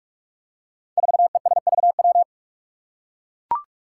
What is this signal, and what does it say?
Morse code, heard as a keyed tone, about 44 wpm, 700 Hz. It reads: 4ESVW
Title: Five Challenging Letter-Number Combinations 45wpm